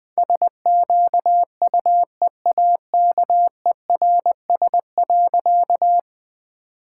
Morse code, heard as a keyed tone, about 20 wpm, 700 Hz. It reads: SQUEAKERS.
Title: All-Age Appropriate Jokes - 20wpm